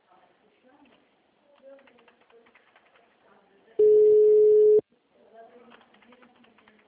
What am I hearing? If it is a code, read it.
Ringback tone (Europe, 425 Hz), with speech.